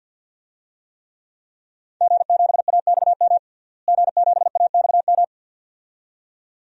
Morse code, heard as a keyed tone, about 51 wpm, 700 Hz. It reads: G7AXM G7AXM
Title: DXCC-100 Call Signs - 5 Characters - Repeated 2x - 50wpm